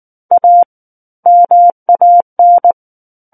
Morse code, heard as a keyed tone, about 19 wpm, 700 Hz. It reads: A MAN
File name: Book One / 10-Chapter 10 - 19 WPM